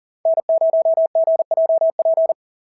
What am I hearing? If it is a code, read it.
Morse code, heard as a keyed tone, about 40 wpm, 690 Hz. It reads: N0GJP
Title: SOTA QSO Practice 40wpm